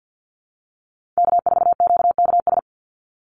Morse code, heard as a keyed tone, about 50 wpm, 700 Hz. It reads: K4QKS
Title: USA Call Signs 50wpm